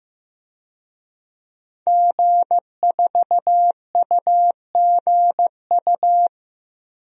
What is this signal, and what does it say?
Morse code, heard as a keyed tone, about 15 wpm, 700 Hz. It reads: G4UGU
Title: DXCC-100 Call Signs - 5 Characters - Repeated 2x - 15wpm